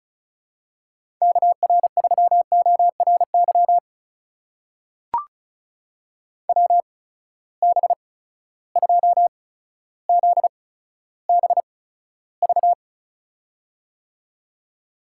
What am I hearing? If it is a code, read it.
Morse code, heard as a keyed tone, about 36 wpm, spaced as for 15 wpm, 700 Hz. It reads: KR3ORY WB2ZBV
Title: USA Call Signs - 6 Characters - 8x Character Spacing - 35wpm